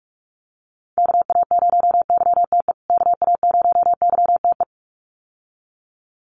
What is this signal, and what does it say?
Morse code, heard as a keyed tone, about 44 wpm, 700 Hz. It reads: KA0YTE KA0YTE
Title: USA Call Signs Repeated 2x 45wpm